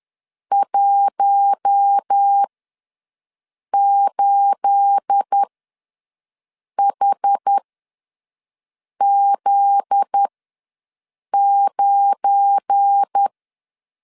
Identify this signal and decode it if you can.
Morse code, heard as a keyed tone, about 10 wpm, spaced as for 5 wpm, 800 Hz. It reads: 18HZ9
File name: LEZIONI / 9 28 ws 10wpm 4124